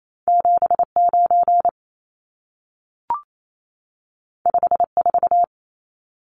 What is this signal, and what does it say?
Morse code, heard as a keyed tone, about 28 wpm, 700 Hz. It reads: 79 54
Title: Numbers - 2-Digit 28wpm